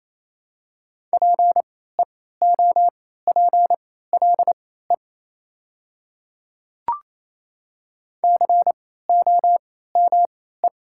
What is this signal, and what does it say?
Morse code, heard as a keyed tone, about 27 wpm, spaced as for 16 wpm, 700 Hz. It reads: PEOPLE COME
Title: 3x Character Spacing - Top 300 Words - 28wpm